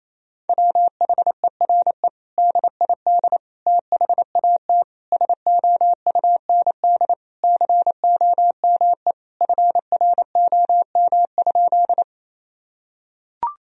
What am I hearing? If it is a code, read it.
Morse code, heard as a keyed tone, about 27 wpm, 700 Hz. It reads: WHERE DID THAT SOUND COME FROM?
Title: Sentences from Top 1000 Words - Review Part 1 - 28wpm